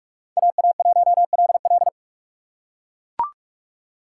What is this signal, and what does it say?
Morse code, heard as a keyed tone, about 41 wpm, 700 Hz. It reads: AA1LL
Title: USA Call Signs - 5 Characters - 2x Character Spacing - 45wpm